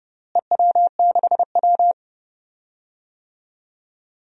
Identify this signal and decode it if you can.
Morse code, heard as a keyed tone, about 32 wpm, 700 Hz. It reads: EW6W
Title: DXCC-100 Call Signs - 4 Characters - Repeated 2x - 30wpm